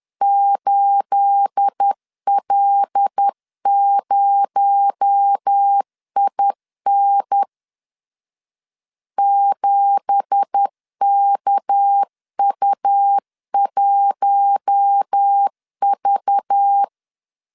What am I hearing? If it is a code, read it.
Morse code, heard as a keyed tone, about 10 wpm, 800 Hz. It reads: 8L0IN 7KU1V
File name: LEZIONI / 9 2867 10wpm 2733